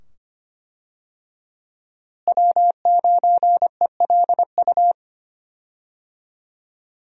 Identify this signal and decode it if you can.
Morse code, heard as a keyed tone, about 25 wpm, 700 Hz. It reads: W9ELU
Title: USA Call Signs - 5 Characters - Repeated 2x - 25wpm